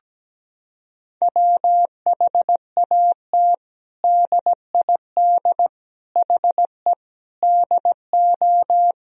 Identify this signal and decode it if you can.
Morse code, heard as a keyed tone, about 17 wpm, 700 Hz. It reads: WHAT DID HE DO
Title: Sentences from Top 100 Words and Abbreviations 17wpm